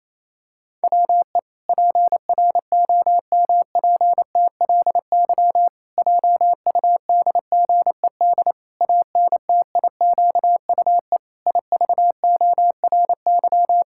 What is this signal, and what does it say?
Morse code, heard as a keyed tone, about 28 wpm, 700 Hz. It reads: WE PROMPTLY JUDGED ANTIQUE IVORY
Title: Pangram Challenge v2 - 28wpm